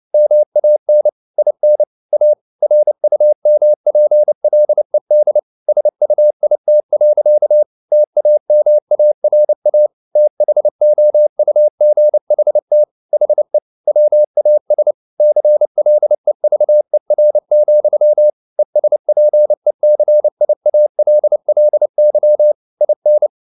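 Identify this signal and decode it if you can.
Morse code, heard as a keyed tone, about 29 wpm, 600 Hz. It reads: MAN IN A RUMPLED SUIT. TAMARA THOUGHT HE WAS CLEVER, ESPECIALLY IN
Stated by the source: Morse code at 29 wpm, sending Ken Follett's novel Never in English, from CW med Gnister.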